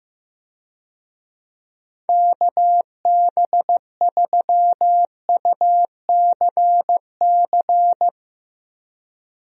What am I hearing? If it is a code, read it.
Morse code, heard as a keyed tone, about 15 wpm, 700 Hz. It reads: KB3UCC
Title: USA Call Signs - 6 Characters - Repeated 3x - 15wpm